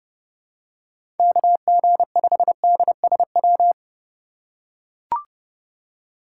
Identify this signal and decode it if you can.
Morse code, heard as a keyed tone, about 30 wpm, 700 Hz. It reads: KG5DSW
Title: USA Call Signs - 6 Characters - 3x Character Spacing - 30wpm